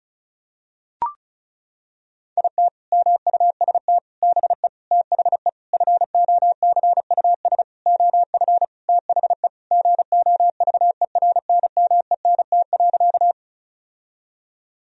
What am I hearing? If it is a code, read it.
Morse code, heard as a keyed tone, about 36 wpm, 700 Hz. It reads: IT MUST BE THE FOCUS OF THE GOVERNMENT.
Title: Sentences from Top 1000 Words - Review Part 4 - 35wpm